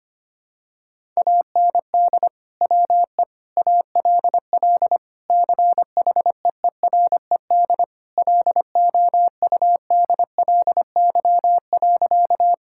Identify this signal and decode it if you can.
Morse code, heard as a keyed tone, about 25 wpm, 700 Hz. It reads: AND WE ALL CHEERED LOUDLY.
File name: practice / 25wpm / Sentences with 2-Word Phrases - 25wpm